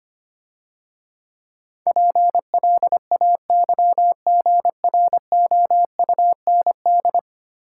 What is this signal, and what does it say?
Morse code, heard as a keyed tone, about 25 wpm, 700 Hz. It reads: PLAYGROUND
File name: Easy Words - Speed-Racing with Hints - 25wpm